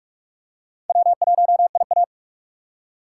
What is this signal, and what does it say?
Morse code, heard as a keyed tone, about 46 wpm, 700 Hz. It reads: W1IA
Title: USA Call Signs - 4 Characters - 4x Character Spacing - 45wpm